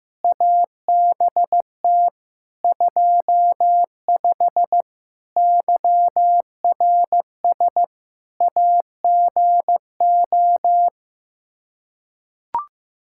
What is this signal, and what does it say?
Morse code, heard as a keyed tone, about 16 wpm, 700 Hz. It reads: ABT 25 YRS AGO
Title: K1USN SST QSO Practice 15wpm